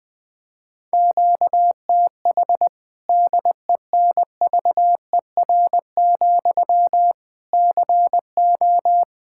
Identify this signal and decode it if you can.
Morse code, heard as a keyed tone, about 20 wpm, 700 Hz. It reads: QTH DENVER, CO
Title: K1USN SST QSO Practice 20wpm